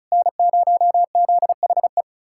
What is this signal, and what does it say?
Morse code, heard as a keyed tone, about 46 wpm, spaced as for 41 wpm, 700 Hz. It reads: N0ZHE
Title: Speed-Racing Common QSO Phrases - 35wpm